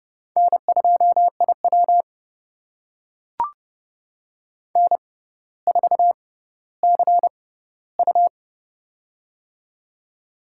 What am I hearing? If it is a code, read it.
Morse code, heard as a keyed tone, about 29 wpm, spaced as for 15 wpm, 700 Hz. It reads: N2IW N4CU
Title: USA Call Signs - 4 Characters - 6x Character Spacing - 30wpm